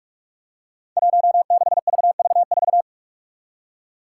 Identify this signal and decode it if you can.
Morse code, heard as a keyed tone, about 44 wpm, 700 Hz. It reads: 16VV4
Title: Five Challenging Letter-Number Combinations 45wpm